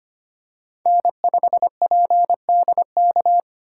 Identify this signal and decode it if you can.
Morse code, heard as a keyed tone, about 25 wpm, 700 Hz. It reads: N5PDK
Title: USA Call Signs - 5 Characters - 4x Character Spacing - 25wpm